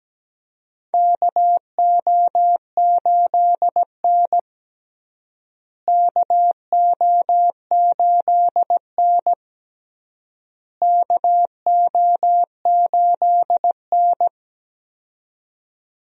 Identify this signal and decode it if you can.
Morse code, heard as a keyed tone, about 17 wpm, 700 Hz. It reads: KO8N KO8N KO8N
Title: USA Call Signs - 4 Characters - Repeated 3x - 17wpm